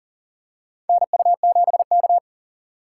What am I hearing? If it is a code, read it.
Morse code, heard as a keyed tone, about 40 wpm, 700 Hz. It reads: NU7K
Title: USA Call Signs - 4 Characters - 6x Character Spacing - 40wpm